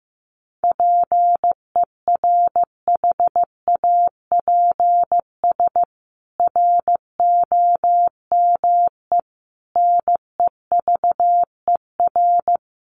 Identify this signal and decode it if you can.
Morse code, heard as a keyed tone, about 15 wpm, 700 Hz. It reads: PERHAPS ROME NEVER
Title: A Sound Of Thunder 15wpm